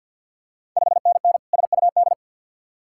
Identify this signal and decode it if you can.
Morse code, heard as a keyed tone, about 51 wpm, 690 Hz. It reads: 5NN SUD
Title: Ontario QSO Party Exchange 50wpm